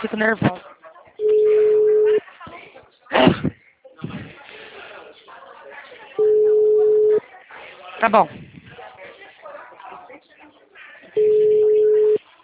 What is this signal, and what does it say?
Ringback tone (Europe, 425 Hz), with speech.